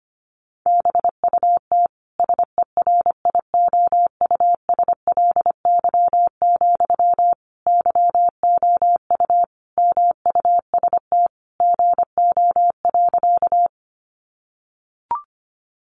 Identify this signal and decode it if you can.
Morse code, heard as a keyed tone, about 25 wpm, 700 Hz. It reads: BUT SERIOUSLY, YOU MUST GO.
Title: Sentences from Top 1200 Words 25wpm